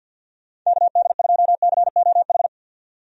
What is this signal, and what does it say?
Morse code, heard as a keyed tone, about 50 wpm, 700 Hz. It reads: KD2COH
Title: USA Call Signs - 6 Characters - 50wpm